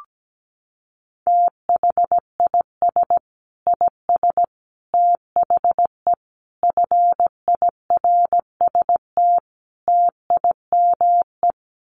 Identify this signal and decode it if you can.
Morse code, heard as a keyed tone, about 17 wpm, 700 Hz. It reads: THIS IS THE FIRST TIME
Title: Sets of 5 Words 17wpm